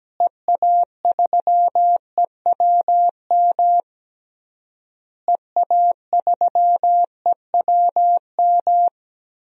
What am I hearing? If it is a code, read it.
Morse code, heard as a keyed tone, about 17 wpm, 700 Hz. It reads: EA3EWM EA3EWM
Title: DXCC-100 Call Signs - 6 Characters - Repeated 3x - 17wpm